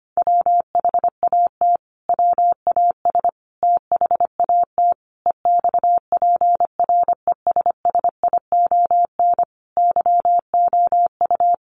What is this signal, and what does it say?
Morse code, heard as a keyed tone, about 25 wpm, 700 Hz. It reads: WHAT WAS THAT EXPRESSION YOU
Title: Sentences from Top 1200 Words 25wpm